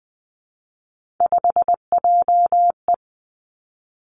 Morse code, heard as a keyed tone, about 20 wpm, 700 Hz. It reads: 5JE
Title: Three Challenging Letter-Number Combinations 20wpm